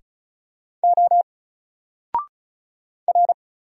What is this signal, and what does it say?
Morse code, heard as a keyed tone, about 33 wpm, 700 Hz. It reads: O R
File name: Single-Letter-Number-35wpm